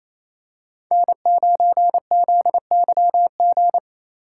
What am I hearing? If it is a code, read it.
Morse code, heard as a keyed tone, about 28 wpm, 700 Hz. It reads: N9ZYG
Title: USA Call Signs - 5 Characters - 4x Character Spacing - 28wpm